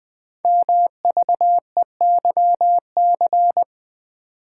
Morse code, heard as a keyed tone, about 20 wpm, 700 Hz. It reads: MVEYC
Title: Code Groups - 5 Letters - 20wpm